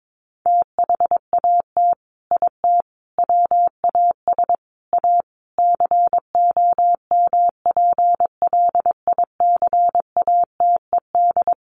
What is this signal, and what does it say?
Morse code, heard as a keyed tone, about 22 wpm, 700 Hz. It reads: THAT IT WAS A COMPLICATED
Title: Sentences from Top 1200 Words 22wpm